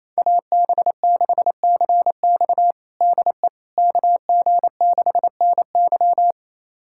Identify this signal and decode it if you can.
Morse code, heard as a keyed tone, about 28 wpm, 700 Hz. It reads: AB6CX DE KG6NY
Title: USA Callsign DE Callsign - 5 Characters - 28wpm